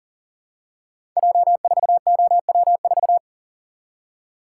Morse code, heard as a keyed tone, about 40 wpm, 700 Hz. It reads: J4OW4
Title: Code Groups - 5 Letter:Numbers - 40wpm